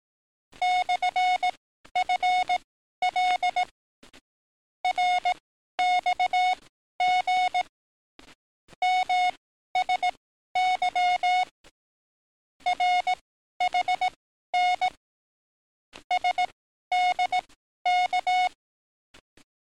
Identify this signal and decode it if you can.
Morse code, heard as a keyed tone, about 18 wpm, spaced as for 12 wpm, 730 Hz. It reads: /FL RXG MSY RHN SDK
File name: training / NAVPERS 11049to11060 / NAVPERS 11052 RA RB